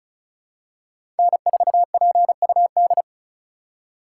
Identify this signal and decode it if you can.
Morse code, heard as a keyed tone, about 36 wpm, 700 Hz. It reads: N4PUD
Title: USA Call Signs - 5 Characters - 3x Character Spacing - 35wpm